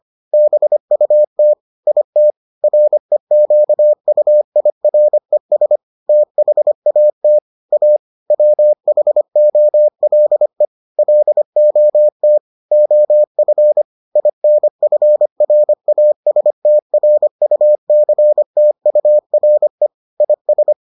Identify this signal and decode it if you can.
Morse code, heard as a keyed tone, about 25 wpm, 600 Hz. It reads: BUT IT REQUIRES THAT A WHOLE LOT OF INFRASTRUCTURE IS